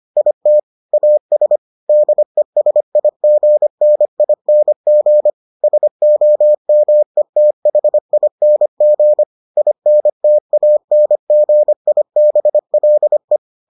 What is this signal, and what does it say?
Morse code, heard as a keyed tone, about 25 wpm, 600 Hz. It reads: IT AS DESIGNING SOMETHING INTANGIBLE